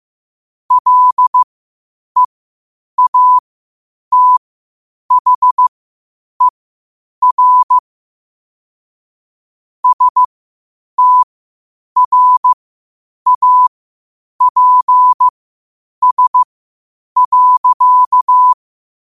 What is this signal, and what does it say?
Morse code, heard as a keyed tone, about 15 wpm, spaced as for 8 wpm, 1000 Hz. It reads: LEATHER STRAPS.
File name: qotdfiles / 8wpm